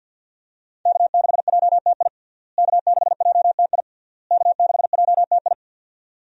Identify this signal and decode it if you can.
Morse code, heard as a keyed tone, about 51 wpm, 700 Hz. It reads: K6JTI K6JTI K6JTI
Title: USA Call Signs - 5 Characters - Repeated 3x - 50wpm